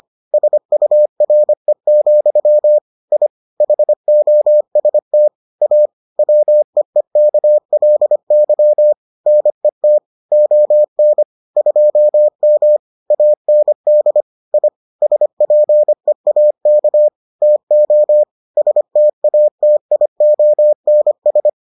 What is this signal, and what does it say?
Morse code, heard as a keyed tone, about 25 wpm, 600 Hz. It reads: SURE, I HOST A WEEKLY NET ON 2M AND I SPEAK TO STATIONS